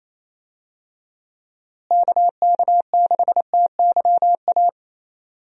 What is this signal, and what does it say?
Morse code, heard as a keyed tone, about 28 wpm, 700 Hz. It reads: KK6TYA
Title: USA Call Signs - 6 Characters - Repeated 2x - 28wpm